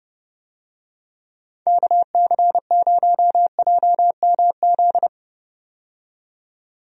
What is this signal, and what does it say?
Morse code, heard as a keyed tone, about 29 wpm, 700 Hz. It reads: KC0JMZ
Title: USA Call Signs - 6 Characters - 30wpm